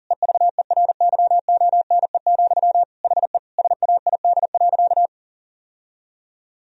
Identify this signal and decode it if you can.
Morse code, heard as a keyed tone, about 40 wpm, 700 Hz. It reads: EVERYONE, HE SAID.
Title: Sentences from Top 1000 Words - Review Part 3 - 40wpm